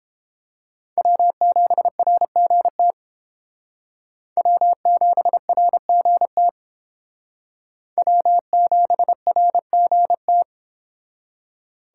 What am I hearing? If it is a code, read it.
Morse code, heard as a keyed tone, about 29 wpm, 700 Hz. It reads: W7RGT W7RGT W7RGT